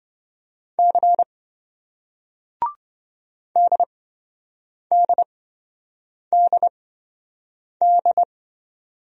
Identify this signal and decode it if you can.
Morse code, heard as a keyed tone, about 28 wpm, spaced as for 7 wpm, 700 Hz. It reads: C DDDD